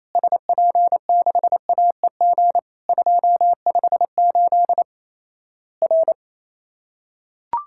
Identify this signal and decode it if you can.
Morse code, heard as a keyed tone, about 28 wpm, 700 Hz. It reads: SP6AEG 258 R
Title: Worked All Europe DX Contest - QTC Exchange - 28wpm